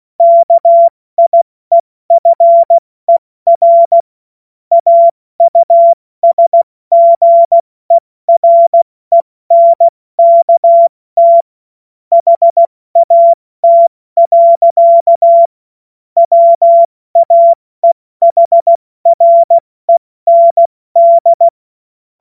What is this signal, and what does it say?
Morse code, heard as a keyed tone, about 16 wpm, 680 Hz. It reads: KIEFER AUSGERENKT HAT. WAEHREND
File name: Morsekurs / Witze / Witz 13. Dezember 2022 70 F80